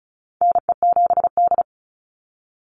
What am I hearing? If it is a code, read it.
Morse code, heard as a keyed tone, about 34 wpm, 700 Hz. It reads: NE7D
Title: Call Signs - 35wpm